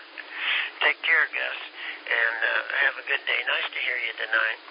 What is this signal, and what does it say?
Emergency Alert System (SAME) tones, heard as harsh buzzing bursts, with speech. It is heard from 0.8 to 4.1 s.